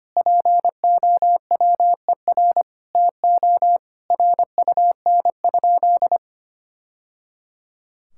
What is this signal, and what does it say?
Morse code, heard as a keyed tone, about 25 wpm, 700 Hz. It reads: POWER TO RUN?
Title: Sentences from Top 1000 Words - Review Part 6 - 25wpm